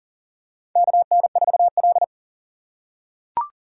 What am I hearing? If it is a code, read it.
Morse code, heard as a keyed tone, about 40 wpm, 700 Hz. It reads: KN4L
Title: USA Call Signs - 4x Character Spacing - 40wpm